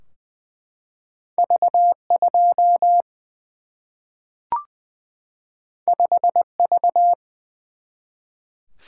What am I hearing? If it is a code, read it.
Morse code, heard as a keyed tone, about 20 wpm, 700 Hz. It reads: V 2 5 V
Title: Single-Letter Single-Number 20wpm